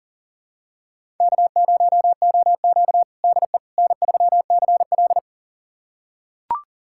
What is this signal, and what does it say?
Morse code, heard as a keyed tone, about 40 wpm, 700 Hz. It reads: K0OQ DE N3CL
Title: USA Callsign DE Callsign - 4 Characters - 40wpm